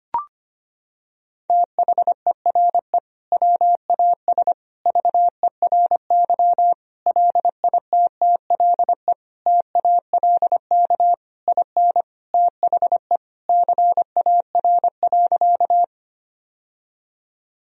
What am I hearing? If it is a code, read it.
Morse code, heard as a keyed tone, about 25 wpm, 700 Hz. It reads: THERE WAS VERY LITTLE TALK IN THE CAR.
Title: Sentences from Top 1000 Words - Review Part 1 - 25wpm